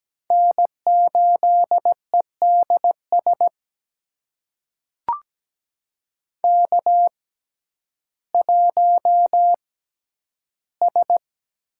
Morse code, heard as a keyed tone, about 17 wpm, spaced as for 10 wpm, 700 Hz. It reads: N8EDS K1S
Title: USA Call Signs - 5 Characters - 6x Character Spacing - 17wpm